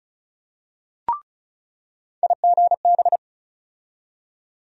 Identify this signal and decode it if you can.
Morse code, heard as a keyed tone, about 36 wpm, 700 Hz. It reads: IGB
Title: USA Call Sign Postfixes - 3 Letters 35wpm